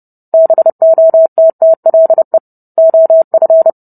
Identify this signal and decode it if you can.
Morse code, heard as a keyed tone, about 30 wpm, 650 Hz. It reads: BOTTLE OF